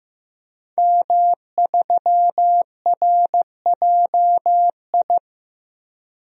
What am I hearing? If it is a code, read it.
Morse code, heard as a keyed tone, about 16 wpm, 700 Hz. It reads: M3RJI
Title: DXCC-100 Call Signs - 5 Characters - Repeated 2x - 15wpm